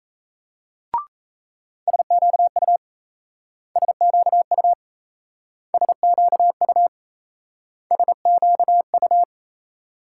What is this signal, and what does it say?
Morse code, heard as a keyed tone, about 36 wpm, 700 Hz. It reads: SQU SQU SQU SQU